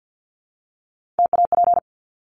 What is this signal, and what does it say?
Morse code, heard as a keyed tone, about 50 wpm, 700 Hz. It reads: TAP
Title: Top 1251-1500 Words 50wpm